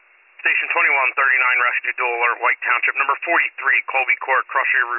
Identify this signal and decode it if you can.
Emergency Alert System (SAME) tones, heard as harsh buzzing bursts, with speech. It is heard from 1.2 to 3.8 s.